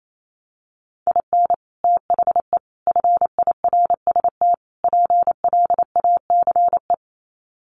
Morse code, heard as a keyed tone, about 28 wpm, 700 Hz. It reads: IN THE FIRST PLACE
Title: Sets of 4 Words 28wpm